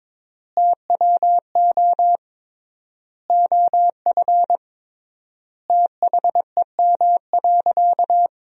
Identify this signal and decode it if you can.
Morse code, heard as a keyed tone, about 22 wpm, 700 Hz. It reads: TWO OF THEM.
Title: 3x Word-Spacing - Sentences from Top 100 Words - 22wpm